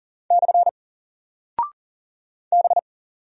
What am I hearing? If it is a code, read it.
Morse code, heard as a keyed tone, about 39 wpm, 700 Hz. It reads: / B
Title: Single Character (Letter:Number:Punctuation:Prosign) 40wpm